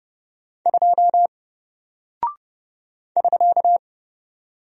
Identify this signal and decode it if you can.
Morse code, heard as a keyed tone, about 30 wpm, 700 Hz. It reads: 2 <SK>
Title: Single Character (Letter:Number:Punctuation:Prosign) Alt 30wpm